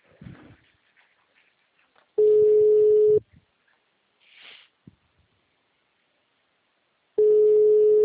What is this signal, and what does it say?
Ringback tone (Europe, 425 Hz).